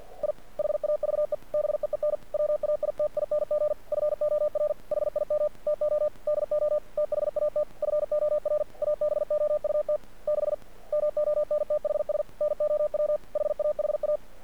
Morse code, heard as a keyed tone, about 46 wpm, 600 Hz. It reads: I HAVE BEEN WAITING FOR HIM TO DO THAT FOR ABOUT 6 MONTHS NOW HAHA
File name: archived / CRSnet